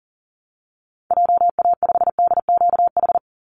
Morse code, heard as a keyed tone, about 40 wpm, 700 Hz. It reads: JA5DQH
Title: Call Signs Repeated 3x - 40wpm